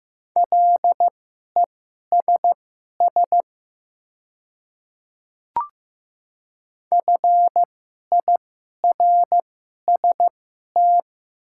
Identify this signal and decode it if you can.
Morse code, heard as a keyed tone, about 15 wpm, spaced as for 11 wpm, 700 Hz. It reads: LESS FIRST
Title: 2x Character Spacing - Top 500 Words - 15wpm